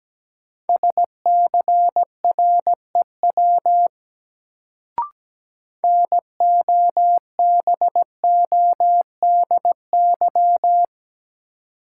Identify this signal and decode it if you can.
Morse code, heard as a keyed tone, about 17 wpm, 700 Hz. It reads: SCREW NOBODY
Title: Sentences from Top 1600 Words - New Word Drill - 17wpm